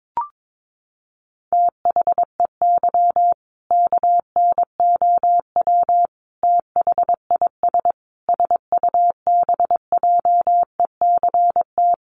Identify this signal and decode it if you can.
Morse code, heard as a keyed tone, about 22 wpm, 700 Hz. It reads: THEY KNOW THIS SUBJECT
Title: Sentences from Top 800 Words 22wpm